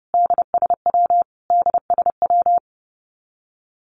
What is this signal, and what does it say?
Morse code, heard as a keyed tone, about 30 wpm, 700 Hz. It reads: DSW DSW
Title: Abbreviations Repeated 3x - 30wpm